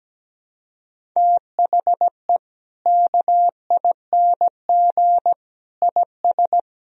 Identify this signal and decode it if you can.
Morse code, heard as a keyed tone, about 17 wpm, 700 Hz. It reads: THE KING IS
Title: Sentences from Top 1000 Words - Review Part 1 - 17wpm